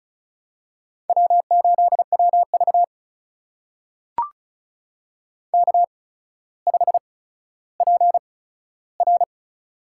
Morse code, heard as a keyed tone, about 36 wpm, spaced as for 15 wpm, 700 Hz. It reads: W8WV K5PR
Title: USA Call Signs - 8x Character Spacing - 35wpm